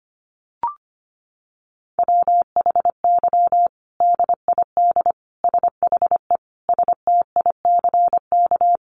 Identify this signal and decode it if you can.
Morse code, heard as a keyed tone, about 25 wpm, 700 Hz. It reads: WHY DID SHE STICK
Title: Sentences from Top 1200 Words 25wpm